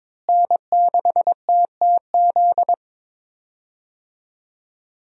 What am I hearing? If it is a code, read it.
Morse code, heard as a keyed tone, about 22 wpm, 700 Hz. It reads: N6TTZ
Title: USA Call Signs - 5 Characters - Repeated 3x - 22wpm